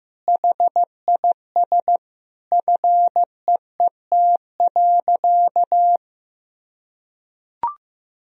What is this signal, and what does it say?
Morse code, heard as a keyed tone, about 14 wpm, 700 Hz. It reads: HIS FEET.
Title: Sentences from Top 1000 Words - Review Part 1 - 15wpm